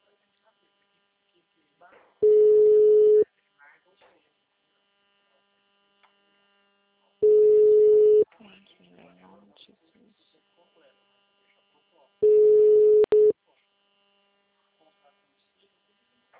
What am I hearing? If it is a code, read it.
Ringback tone (Europe, 425 Hz), with speech.
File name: out- -2010- - - .44818